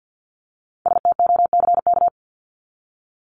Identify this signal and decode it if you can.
Morse code, heard as a keyed tone, about 50 wpm, 700 Hz. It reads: STOCK
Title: Top 501-750 Words 50wpm